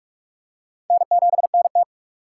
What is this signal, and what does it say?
Morse code, heard as a keyed tone, about 42 wpm, 700 Hz. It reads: N7NT
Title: K1USN SST QSO Practice - 45wpm